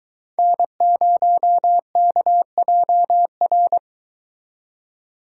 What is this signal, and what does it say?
Morse code, heard as a keyed tone, about 23 wpm, 700 Hz. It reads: N0KJR